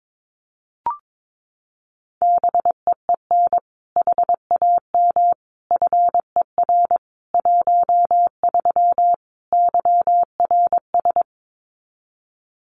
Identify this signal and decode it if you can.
Morse code, heard as a keyed tone, about 22 wpm, 700 Hz. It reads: BEEN HAM FER 13 YRS
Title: Common QSO Phrases 22wpm